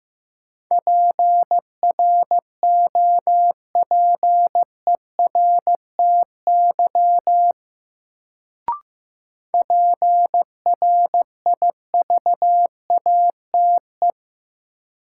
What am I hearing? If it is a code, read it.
Morse code, heard as a keyed tone, about 15 wpm, 700 Hz. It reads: PROPERTY PRIVATE
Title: Sentences from Top 1000 Words - New Word Drill - 15wpm